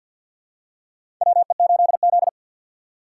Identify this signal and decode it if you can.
Morse code, heard as a keyed tone, about 53 wpm, 700 Hz. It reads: WE8Z
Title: USA Call Signs - 4 Characters - 2x Character Spacing - 50wpm